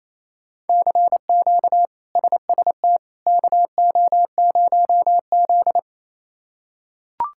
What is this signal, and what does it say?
Morse code, heard as a keyed tone, about 28 wpm, 700 Hz. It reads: CQ SST KO0Z
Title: K1USN SST QSO Practice - 28wpm